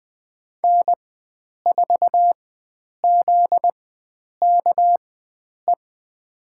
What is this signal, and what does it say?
Morse code, heard as a keyed tone, about 21 wpm, spaced as for 9 wpm, 700 Hz. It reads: N4ZKE
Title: USA Call Signs - 5 Characters - 4x Character Spacing - 20wpm